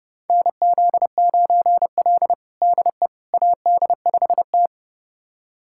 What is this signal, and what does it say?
Morse code, heard as a keyed tone, about 30 wpm, 700 Hz. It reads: NZ9L DE AD5T
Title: USA Callsign DE Callsign - 4 Characters - 30wpm